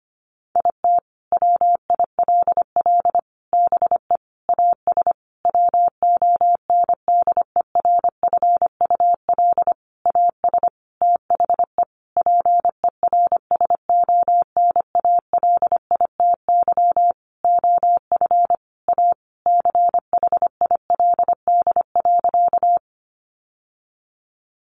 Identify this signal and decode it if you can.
Morse code, heard as a keyed tone, about 25 wpm, 700 Hz. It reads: IT WILL BE AS WONDERFUL AS THE PERSONALITY OF A CHILD.
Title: Sentences from Top 1400 Words 25wpm